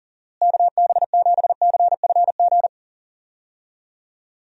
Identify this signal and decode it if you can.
Morse code, heard as a keyed tone, about 40 wpm, 700 Hz. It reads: KB7CFG
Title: USA Call Signs - 6 Characters - Repeated 3x - 40wpm